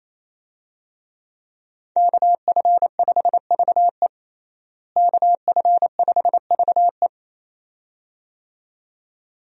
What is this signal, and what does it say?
Morse code, heard as a keyed tone, about 28 wpm, 700 Hz. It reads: KF5VE KF5VE
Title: USA Call Signs - 5 Characters - Repeated 2x - 28wpm